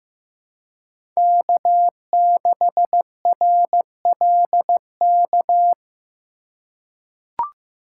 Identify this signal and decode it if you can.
Morse code, heard as a keyed tone, about 15 wpm, 700 Hz. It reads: K6RLK
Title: USA Call Signs - 5 Characters - 3x Character Spacing - 15wpm